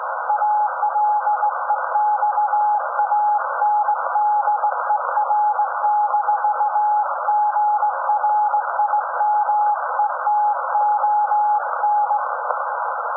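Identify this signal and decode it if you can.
Morse code, heard as a keyed tone, about 18 wpm, 840 Hz. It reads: ALEXANDER-GRIMETON
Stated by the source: SAQ Grimeton Radio, Alexanderson 200 kW alternator on 17.2 kHz, received in Potsdam with a very strong signal on World Radio Day.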